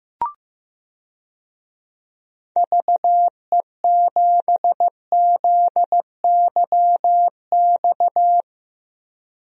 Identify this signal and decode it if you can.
Morse code, heard as a keyed tone, about 15 wpm, 700 Hz. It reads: VE7ZYX
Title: DXCC-100 Call Signs - 6 Characters - Repeated 2x - 15wpm